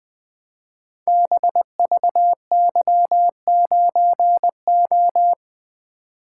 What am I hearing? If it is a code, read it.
Morse code, heard as a keyed tone, about 20 wpm, 700 Hz. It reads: BVY9O
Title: Code Groups - 5 Letter:Numbers - 20wpm